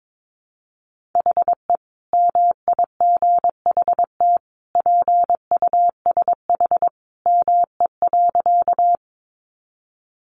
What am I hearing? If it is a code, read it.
Morse code, heard as a keyed tone, about 22 wpm, 700 Hz. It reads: HE MIGHT PUSH ME.
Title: Sentences from Top 1300 Words 22wpm